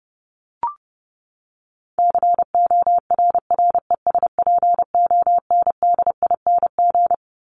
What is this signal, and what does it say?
Morse code, heard as a keyed tone, about 30 wpm, 700 Hz. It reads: CORRESPONDING
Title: ING Suffix - Top 100 Words with Base Words 30wpm